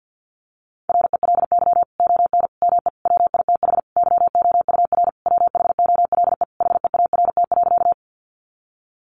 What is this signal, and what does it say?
Morse code, heard as a keyed tone, about 50 wpm, 700 Hz. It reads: RELY ON ME WITH YOUR WHOLE HEART.
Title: Sentences from Top 1200 Words 50wpm